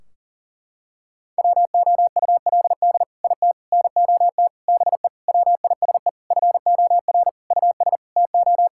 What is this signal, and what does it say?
Morse code, heard as a keyed tone, about 40 wpm, 700 Hz. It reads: WOULD IT NOT BE WISE FOR US TO
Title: Sentences from Top 1600 Words 40wpm